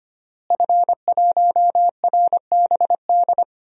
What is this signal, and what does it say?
Morse code, heard as a keyed tone, about 25 wpm, 700 Hz. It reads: F1RBD
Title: DXCC-100 Call Signs - 5 Characters - Repeated 2x - 25wpm